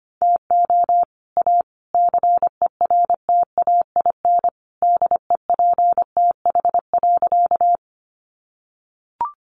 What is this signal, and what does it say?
Morse code, heard as a keyed tone, about 25 wpm, 700 Hz. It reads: TO A CERTAIN DEPTH.
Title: Sentences from Top 1200 Words 25wpm